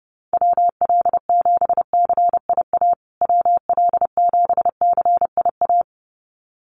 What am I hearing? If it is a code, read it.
Morse code, heard as a keyed tone, about 30 wpm, 700 Hz. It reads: WL7CIA WL7CIA
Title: USA Call Signs Repeated 2x 30wpm